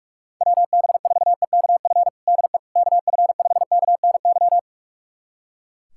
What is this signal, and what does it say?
Morse code, heard as a keyed tone, about 41 wpm, 700 Hz. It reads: WB4EKF DE KF5KNY
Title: USA Callsign DE Callsign - 6 Characters - 45wpm